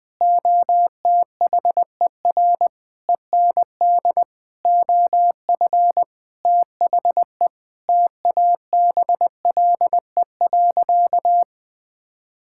Morse code, heard as a keyed tone, about 19 wpm, 700 Hz. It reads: OTHER END OF THE TABLE.
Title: Sentences with 6-Word Phrases - 20wpm